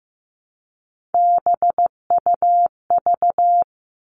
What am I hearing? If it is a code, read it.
Morse code, heard as a keyed tone, about 15 wpm, 700 Hz. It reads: BUV
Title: All Cut Numbers - 3-Digit 15wpm